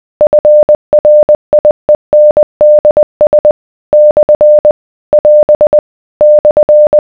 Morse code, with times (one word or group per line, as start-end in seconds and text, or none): 0.2-3.5 FRIENDS
3.9-4.7 /
5.1-5.8 <AS>
6.2-7.0 /